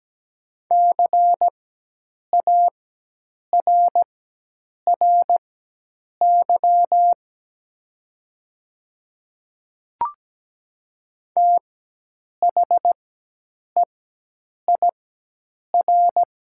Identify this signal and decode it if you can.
Morse code, heard as a keyed tone, about 17 wpm, spaced as for 8 wpm, 700 Hz. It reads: CARRY THEIR
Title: 4x Character Spacing - Top 300 Words - 17wpm